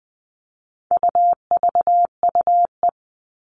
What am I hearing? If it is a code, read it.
Morse code, heard as a keyed tone, about 20 wpm, 700 Hz. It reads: UVUE
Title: All Cut Numbers - 4-Digit 20wpm